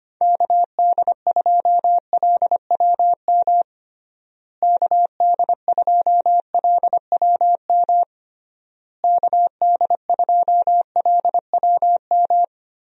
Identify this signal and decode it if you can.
Morse code, heard as a keyed tone, about 25 wpm, 700 Hz. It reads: KD2LWM KD2LWM KD2LWM
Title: USA Call Signs - 6 Characters - Repeated 3x - 25wpm